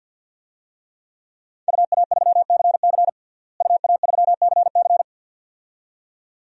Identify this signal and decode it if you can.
Morse code, heard as a keyed tone, about 53 wpm, 700 Hz. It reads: UA3CC UA3CC
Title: DXCC-100 Call Signs - 5 Characters - Repeated 2x - 50wpm